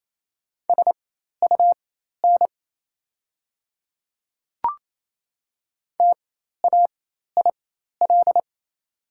Morse code, heard as a keyed tone, about 28 wpm, spaced as for 13 wpm, 700 Hz. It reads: SUN TAIL
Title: 4x Character Spacing - Top 500 Words - 28wpm